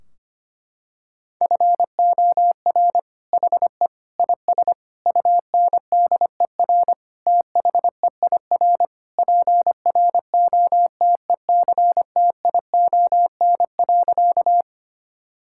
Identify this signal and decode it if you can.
Morse code, heard as a keyed tone, about 25 wpm, 700 Hz. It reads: FOR HE IS UNDER THEIR PROTECTION.
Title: Sentences from Top 1600 Words 25wpm